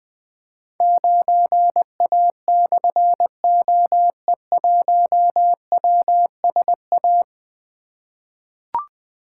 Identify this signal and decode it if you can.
Morse code, heard as a keyed tone, about 20 wpm, 700 Hz. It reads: 9A/OE1WSA
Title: Difficult Call Signs 20wpm